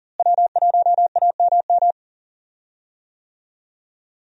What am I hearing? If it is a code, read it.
Morse code, heard as a keyed tone, about 40 wpm, 700 Hz. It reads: W1AMM
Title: USA Call Signs - 5 Characters - Repeated 3x - 40wpm